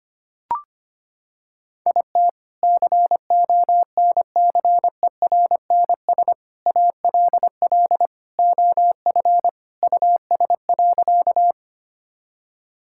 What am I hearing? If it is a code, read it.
Morse code, heard as a keyed tone, about 25 wpm, 700 Hz. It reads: IT CONCERNS ALL OF US.
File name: practice / 25wpm / Sentences with 3-Word Phrases - 25wpm